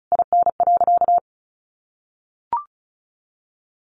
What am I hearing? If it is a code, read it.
Morse code, heard as a keyed tone, about 34 wpm, 700 Hz. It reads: IN.
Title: Sentences from Top 1300 Words 35wpm